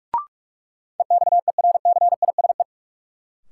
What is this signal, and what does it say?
Morse code, heard as a keyed tone, about 46 wpm, 700 Hz. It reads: EXERCISE
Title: Sentences from Top 1000 Words - Review Part 4 - New Word Drill - 45wpm